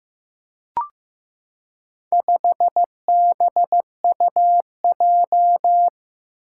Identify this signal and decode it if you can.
Morse code, heard as a keyed tone, about 15 wpm, 700 Hz. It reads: 5BUJ
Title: Four Challenging Letter-Number Combinations 15wpm